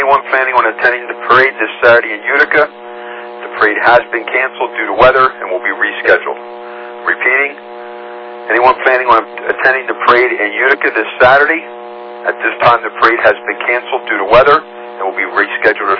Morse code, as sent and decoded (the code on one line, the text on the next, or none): none